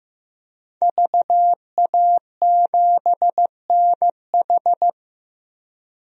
Morse code, heard as a keyed tone, about 15 wpm, 700 Hz. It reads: VA7NH
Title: DXCC-100 Call Signs - 5 Characters - 15wpm